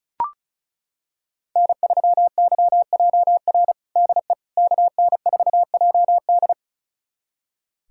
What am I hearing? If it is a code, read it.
Morse code, heard as a keyed tone, about 36 wpm, 700 Hz. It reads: N3YJR DE KN4JD
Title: USA Callsign DE Callsign - 5 Characters - 35wpm